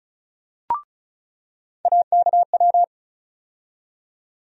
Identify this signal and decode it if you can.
Morse code, heard as a keyed tone, about 36 wpm, 700 Hz. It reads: AKW
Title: USA Call Sign Postfixes - 3 Letters 35wpm